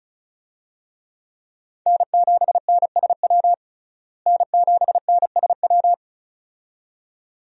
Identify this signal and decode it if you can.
Morse code, heard as a keyed tone, about 34 wpm, 700 Hz. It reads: N7NSW N7NSW
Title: USA Call Signs - 5 Characters - Repeated 2x - 35wpm